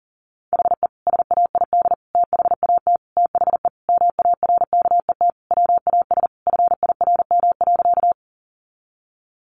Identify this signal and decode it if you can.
Morse code, heard as a keyed tone, about 37 wpm, 700 Hz. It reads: HE SAID THAT THE MARKET WAS FIRM.
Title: Sentences from Top 1400 Words 40wpm